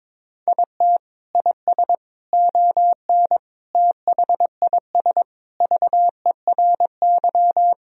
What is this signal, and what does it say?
Morse code, heard as a keyed tone, about 22 wpm, 700 Hz. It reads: IT IS ON THIS VERY
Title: Sentences from Top 1000 Words - Review Part 1 - 22wpm